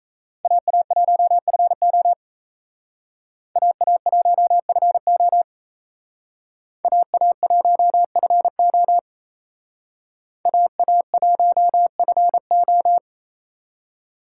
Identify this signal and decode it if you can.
Morse code, heard as a keyed tone, about 35 wpm, 700 Hz. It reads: AA1FO AA1FO AA1FO AA1FO
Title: Speed Racing - USA 2x2 Call Signs 28wpm